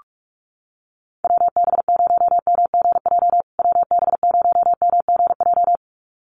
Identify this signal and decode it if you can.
Morse code, heard as a keyed tone, about 44 wpm, 700 Hz. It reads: WB0MGJ WB0MGJ
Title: USA Call Signs Repeated 2x 45wpm